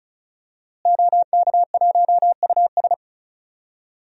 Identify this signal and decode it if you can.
Morse code, heard as a keyed tone, about 36 wpm, 700 Hz. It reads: OK1US
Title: DXCC-100 Call Signs - 5 Characters - Repeated 3x - 35wpm